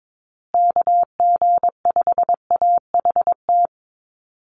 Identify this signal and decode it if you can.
Morse code, heard as a keyed tone, about 22 wpm, 700 Hz. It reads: KG5AHT
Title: USA Call Signs Repeated 3x 22wpm